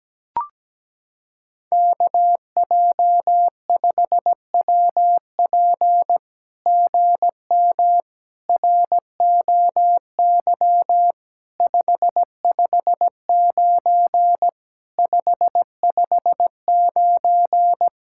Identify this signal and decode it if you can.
Morse code, heard as a keyed tone, about 17 wpm, 700 Hz. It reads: KJ5WP GM ROY 559 559
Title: SOTA QSO Practice 17wpm